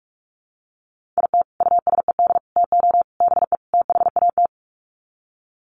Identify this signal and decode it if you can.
Morse code, heard as a keyed tone, about 44 wpm, 700 Hz. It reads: IT USED TO BE THAT
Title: Sets of 5 Words Encore2 45wpm